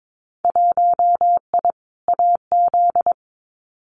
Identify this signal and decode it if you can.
Morse code, heard as a keyed tone, about 22 wpm, 700 Hz. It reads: 1I AZ
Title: Winter Field Day Exchanges 22wpm